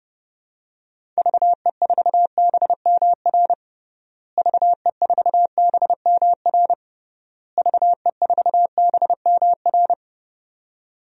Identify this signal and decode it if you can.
Morse code, heard as a keyed tone, about 31 wpm, 700 Hz. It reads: VE4BMR VE4BMR VE4BMR
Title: DXCC-100 Call Signs - 6 Characters - Repeated 3x - 30wpm